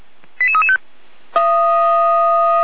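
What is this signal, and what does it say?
Five-tone selective call (ZVEI), heard as quick runs of five tones, tones of about 70 ms. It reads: 88335
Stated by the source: ZVEI five-tone selective call received on SDR.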